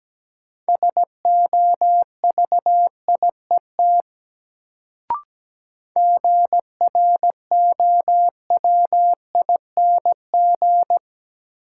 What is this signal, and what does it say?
Morse code, heard as a keyed tone, about 17 wpm, 700 Hz. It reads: SOVIET GROWING
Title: Sets of 5 Words Encore - New Word Drill - 17wpm